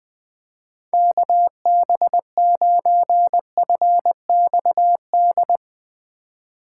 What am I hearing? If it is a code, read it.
Morse code, heard as a keyed tone, about 21 wpm, 700 Hz. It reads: KB9FXD
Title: USA Call Signs - 6 Characters - 8x Character Spacing - 20wpm